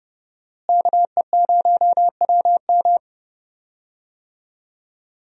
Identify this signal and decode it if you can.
Morse code, heard as a keyed tone, about 30 wpm, 700 Hz. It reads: KE0WM
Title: USA Call Signs - 5 Characters - Repeated 3x - 30wpm